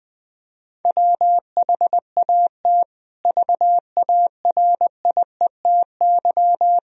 Morse code, heard as a keyed tone, about 20 wpm, 700 Hz. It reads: WHAT VARIETY
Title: Sentences from Top 1000 Words - Review Part 4 - 20wpm